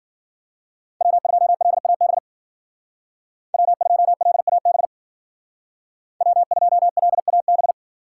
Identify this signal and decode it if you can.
Morse code, heard as a keyed tone, about 53 wpm, 700 Hz. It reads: W2LAB W2LAB W2LAB